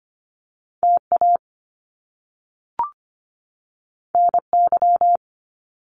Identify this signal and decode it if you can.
Morse code, heard as a keyed tone, about 25 wpm, 700 Hz. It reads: TA NY
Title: Top Two Letter Combinations in Words 25wpm